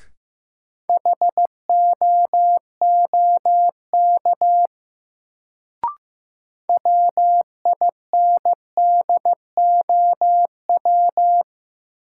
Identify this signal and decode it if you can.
Morse code, heard as a keyed tone, about 15 wpm, 700 Hz. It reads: HOOK WINDOW
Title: Sentences from Top 1600 Words - New Word Drill - 15wpm